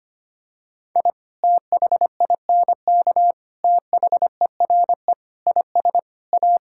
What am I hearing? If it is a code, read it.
Morse code, heard as a keyed tone, about 25 wpm, 700 Hz. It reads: I THINK THERE IS A
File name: practice / 25wpm / Sentences with 3-Word Phrases - 25wpm